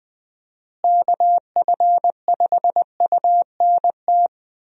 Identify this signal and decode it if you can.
Morse code, heard as a keyed tone, about 20 wpm, 700 Hz. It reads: KF5UNT
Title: USA Call Signs - 6 Characters - Repeated 3x - 20wpm